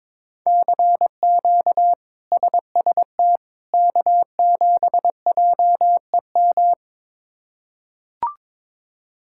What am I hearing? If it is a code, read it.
Morse code, heard as a keyed tone, about 22 wpm, 700 Hz. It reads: CQ SST K7JEM
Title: K1USN SST QSO Practice - 22wpm